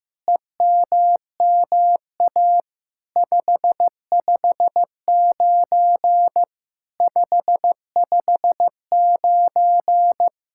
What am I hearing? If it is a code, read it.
Morse code, heard as a keyed tone, about 16 wpm, 700 Hz. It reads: EMMA 559 559
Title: SOTA QSO Practice 15wpm